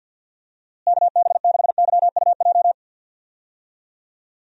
Morse code, heard as a keyed tone, about 46 wpm, 700 Hz. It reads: KB6YUJ
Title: USA Call Signs - 6 Characters - Repeated 2x - 50wpm